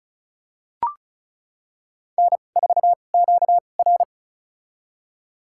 Morse code, heard as a keyed tone, about 36 wpm, spaced as for 25 wpm, 700 Hz. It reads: N4QR
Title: USA Call Signs - 2x Character Spacing - 35wpm